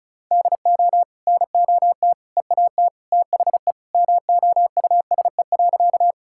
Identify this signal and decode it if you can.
Morse code, heard as a keyed tone, about 36 wpm, 700 Hz. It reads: DO NOT EAT THE MOUSE.
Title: Sentences from Top 1000 Words - Review Part 5 - 35wpm